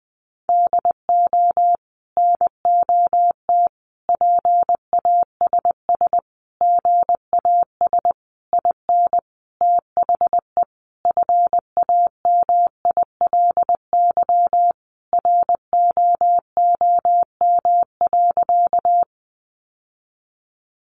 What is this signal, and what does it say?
Morse code, heard as a keyed tone, about 20 wpm, 700 Hz. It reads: DO NOT PASS GAS IN THE FAMILY ROOM.
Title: Sentences from Top 1100 Words 20wpm